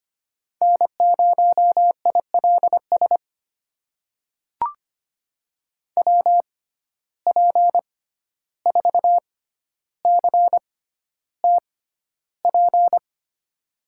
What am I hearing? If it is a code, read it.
Morse code, heard as a keyed tone, about 25 wpm, spaced as for 12 wpm, 700 Hz. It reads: N0ILS WP4CTP
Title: USA Call Signs - 6x Character Spacing - 25wpm